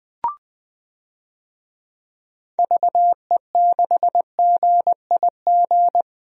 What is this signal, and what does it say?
Morse code, heard as a keyed tone, about 21 wpm, 700 Hz. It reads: VE6GIG
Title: DXCC-100 Call Signs - 6 Characters - Repeated 3x - 20wpm